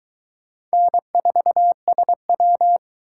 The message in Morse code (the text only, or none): N4SW